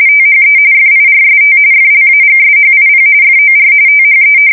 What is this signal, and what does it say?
RTTY, about 45.45 baud. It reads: THE FIRST FLAGSHIP FOR THE
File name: fsk170 / battleshipnj